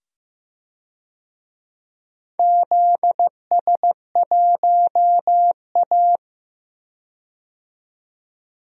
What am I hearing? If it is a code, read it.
Morse code, heard as a keyed tone, about 15 wpm, 700 Hz. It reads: ZS1A
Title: DXCC-100 Call Signs - 4 Characters - 15wpm